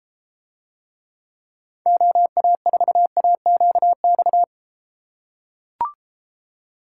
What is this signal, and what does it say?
Morse code, heard as a keyed tone, about 33 wpm, 700 Hz. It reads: OA4AQX